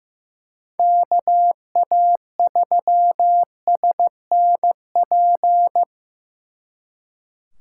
Morse code, heard as a keyed tone, about 18 wpm, 700 Hz. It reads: KA3SNP
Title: Speed Racing - USA 2x3 Call Signs 15wpm